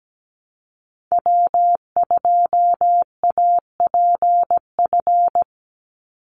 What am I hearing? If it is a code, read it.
Morse code, heard as a keyed tone, about 17 wpm, 700 Hz. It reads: W2APF
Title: Call Signs Repeated 3x - 17wpm